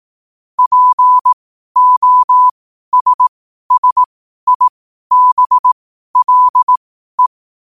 Morse code, heard as a keyed tone, about 18 wpm, spaced as for 12 wpm, 1000 Hz. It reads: POSSIBLE
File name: qotdfiles / 12wpm